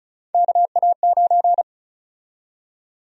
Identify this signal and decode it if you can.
Morse code, heard as a keyed tone, about 36 wpm, 700 Hz. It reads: KA9
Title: USA Call Sign Prefixes - 35wpm